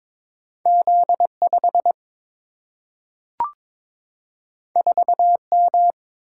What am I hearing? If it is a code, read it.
Morse code, heard as a keyed tone, about 22 wpm, 700 Hz. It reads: Z5 4M
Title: Single-Letter Single-Number 22wpm